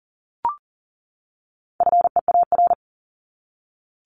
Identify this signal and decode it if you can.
Morse code, heard as a keyed tone, about 40 wpm, 700 Hz. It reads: FEAR
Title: Top 1751-2000 Words 40wpm